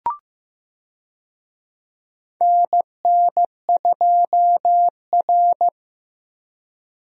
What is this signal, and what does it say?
Morse code, heard as a keyed tone, about 15 wpm, 700 Hz. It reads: NN2R
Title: USA Call Signs - 4 Characters - Repeated 2x - 15wpm